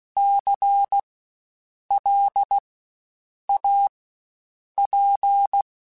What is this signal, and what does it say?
Morse code, heard as a keyed tone, about 16 wpm, spaced as for 7 wpm, 800 Hz. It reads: CLAP
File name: Spaced Alphas / Spaced Alphas mp3s / Cs sp @15